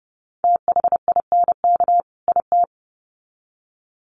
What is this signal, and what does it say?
Morse code, heard as a keyed tone, about 30 wpm, 700 Hz. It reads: THINK IT
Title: Sets of 2 Words 30wpm